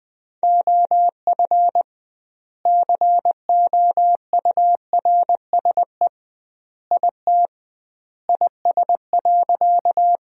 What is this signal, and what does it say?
Morse code, heard as a keyed tone, about 19 wpm, 700 Hz. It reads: OF COURSE IT IS.
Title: 2x Word-Spacing - Sentences from Top 500 Words - 20wpm